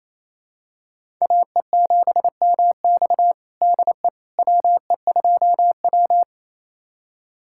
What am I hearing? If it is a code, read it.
Morse code, heard as a keyed tone, about 28 wpm, 700 Hz. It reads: AE7MX DE WE2W
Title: USA Callsign DE Callsign 28wpm